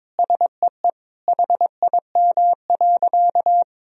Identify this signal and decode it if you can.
Morse code, heard as a keyed tone, about 22 wpm, 700 Hz. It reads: SEE HIM.
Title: Sentences from Top 1000 Words - Review Part 3 - 22wpm